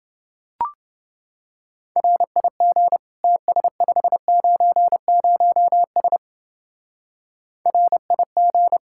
Morse code, heard as a keyed tone, about 27 wpm, 700 Hz. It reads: RIG TS590S RIG